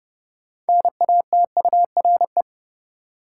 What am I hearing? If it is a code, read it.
Morse code, heard as a keyed tone, about 30 wpm, 700 Hz. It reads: NATURE
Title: Sentences from Top 600 Words - New Word Drill - 30wpm